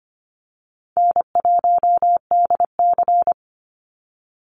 Morse code, heard as a keyed tone, about 25 wpm, 700 Hz. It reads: N1DC
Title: Easy Call Signs - 25wpm